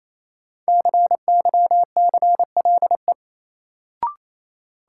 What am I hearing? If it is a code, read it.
Morse code, heard as a keyed tone, about 28 wpm, 700 Hz. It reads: CYCLE
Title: Sentences from Top 900 Words - New Word Drill - 28wpm